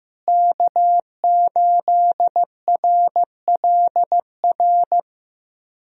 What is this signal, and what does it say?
Morse code, heard as a keyed tone, about 16 wpm, 700 Hz. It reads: K8RLR
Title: USA Call Signs - 5 Characters - Repeated 3x - 15wpm